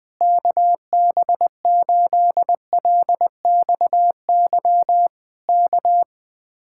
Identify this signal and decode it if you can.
Morse code, heard as a keyed tone, about 20 wpm, 700 Hz. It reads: KB8LXY K
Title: Speed-Racing Common QSO Phrases - 17wpm